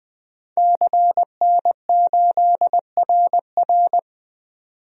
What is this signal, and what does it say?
Morse code, heard as a keyed tone, about 20 wpm, 700 Hz. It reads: CN8RR
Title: DXCC-100 Call Signs - 5 Characters - 20wpm